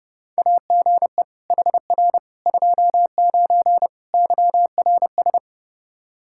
Morse code, heard as a keyed tone, about 32 wpm, 700 Hz. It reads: AGE HR 29 YRS
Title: Practice QSO - Exchange 3 - More Info - 30wpm